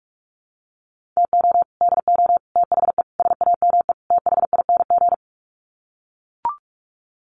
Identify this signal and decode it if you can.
Morse code, heard as a keyed tone, about 44 wpm, 700 Hz. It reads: TO DO THE SAME THING
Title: Sets of 5 Words Encore 45wpm